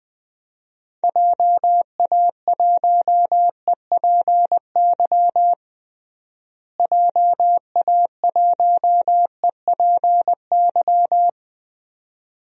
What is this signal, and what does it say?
Morse code, heard as a keyed tone, about 21 wpm, 700 Hz. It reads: JA1EPY JA1EPY
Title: DXCC-100 Call Signs - 6 Characters - Repeated 2x - 20wpm